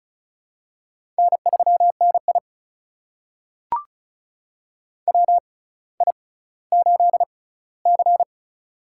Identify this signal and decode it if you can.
Morse code, heard as a keyed tone, about 36 wpm, spaced as for 18 wpm, 700 Hz. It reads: N3NI WI8C
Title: USA Call Signs - 4 Characters - 6x Character Spacing - 35wpm